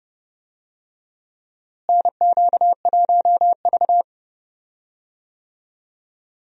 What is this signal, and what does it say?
Morse code, heard as a keyed tone, about 32 wpm, 700 Hz. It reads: NQ1V
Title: USA Call Signs - 4 Characters - Repeated 3x - 30wpm